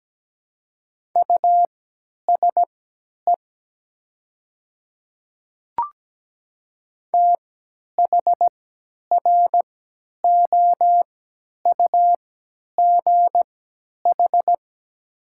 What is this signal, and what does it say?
Morse code, heard as a keyed tone, about 17 wpm, spaced as for 10 wpm, 700 Hz. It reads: USE THROUGH
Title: 3x Character Spacing - Top 500 Words - 17wpm